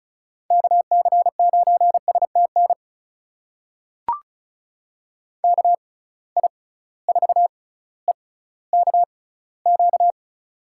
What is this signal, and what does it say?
Morse code, heard as a keyed tone, about 36 wpm, spaced as for 18 wpm, 700 Hz. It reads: KC9STN KI4EKQ
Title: USA Call Signs - 6 Characters - 6x Character Spacing - 35wpm